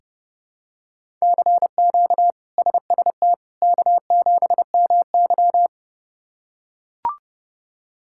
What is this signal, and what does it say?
Morse code, heard as a keyed tone, about 30 wpm, 700 Hz. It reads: CQ SST K7MY
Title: K1USN SST QSO Practice - 30wpm